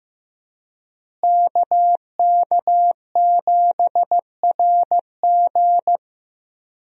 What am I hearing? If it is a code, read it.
Morse code, heard as a keyed tone, about 16 wpm, 700 Hz. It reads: KK7RG
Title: USA Call Signs - 5 Characters - Repeated 3x - 15wpm